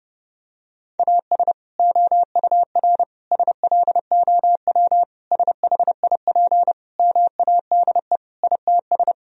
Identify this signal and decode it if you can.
Morse code, heard as a keyed tone, about 32 wpm, 700 Hz. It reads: AS OUR SLOW SHIP MADE ITS
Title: Sentences from Top 1000 Words - Review Part 2 - 30wpm